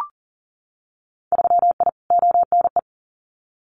Morse code, heard as a keyed tone, about 38 wpm, 700 Hz. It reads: 3I ONE
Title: Winter Field Day Exchanges Encore 40wpm